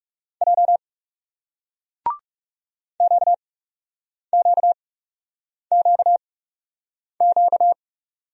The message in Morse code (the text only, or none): J QQQQ